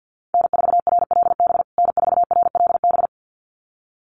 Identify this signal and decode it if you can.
Morse code, heard as a keyed tone, about 50 wpm, 700 Hz. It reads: N4RLB N4RLB
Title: USA Call Signs Repeated 2x 50wpm